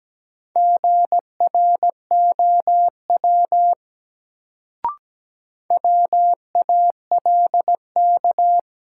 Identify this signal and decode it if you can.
Morse code, heard as a keyed tone, about 17 wpm, 700 Hz. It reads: GROW WALK
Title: Sentences from Top 300 Words - New Word Drill - 17wpm